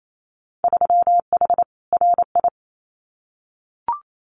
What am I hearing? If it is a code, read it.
Morse code, heard as a keyed tone, about 28 wpm, 700 Hz. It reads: 3H RI
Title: Winter Field Day Exchanges Encore 28wpm